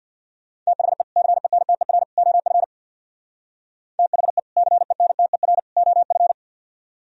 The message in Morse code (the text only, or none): THE CENTER OF THE CENTER OF